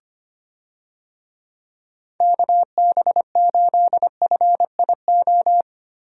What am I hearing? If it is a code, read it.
Morse code, heard as a keyed tone, about 25 wpm, 700 Hz. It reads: KB8FIO
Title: USA Call Signs - 6 Characters - 25wpm